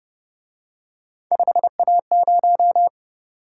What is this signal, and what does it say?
Morse code, heard as a keyed tone, about 28 wpm, 700 Hz. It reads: 5A0